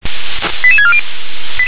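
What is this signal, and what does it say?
Five-tone selective call (ZVEI), heard as quick runs of five tones, tones of about 70 ms. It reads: 88640
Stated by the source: ZVEI five-tone selective call received on SDR.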